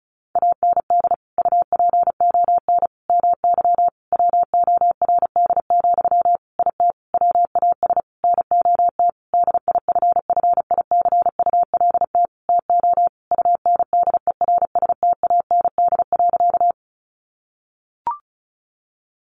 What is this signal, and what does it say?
Morse code, heard as a keyed tone, about 34 wpm, 700 Hz. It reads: AND UPON MY WORD, IT WAS NOT DIFFICULT TO UNDERSTAND.
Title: Sentences from Top 1100 Words 35wpm